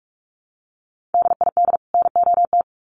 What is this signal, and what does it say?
Morse code, heard as a keyed tone, about 44 wpm, 700 Hz. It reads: DID NOT
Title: Sets of 2 Words 45wpm